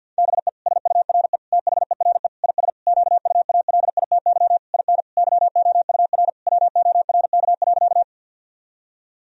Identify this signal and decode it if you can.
Morse code, heard as a keyed tone, about 50 wpm, 700 Hz. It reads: BE SURE THERE IS QUALITY IN YOUR WORK.
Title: Sentences from Top 1000 Words - Review Part 4 - 50wpm